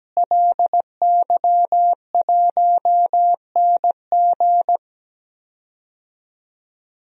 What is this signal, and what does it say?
Morse code, heard as a keyed tone, about 17 wpm, 700 Hz. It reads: LY1NG
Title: DXCC-100 Call Signs - 5 Characters - Repeated 3x - 17wpm